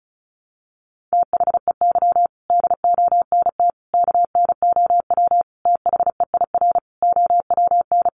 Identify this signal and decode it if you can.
Morse code, heard as a keyed tone, about 34 wpm, 700 Hz. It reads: THEY DONT KNOW THEIR OWN
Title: Sentences from Top 300 Words 35wpm